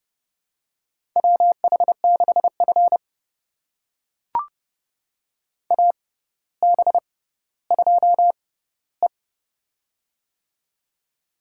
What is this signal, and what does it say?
Morse code, heard as a keyed tone, about 30 wpm, spaced as for 15 wpm, 700 Hz. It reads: WH6F AB2E
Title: USA Call Signs - 4 Characters - 6x Character Spacing - 30wpm